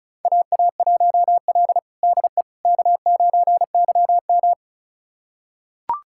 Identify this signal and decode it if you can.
Morse code, heard as a keyed tone, about 36 wpm, 700 Hz. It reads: AA1L DE K9YM
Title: USA Callsign DE Callsign - 4 Characters - 35wpm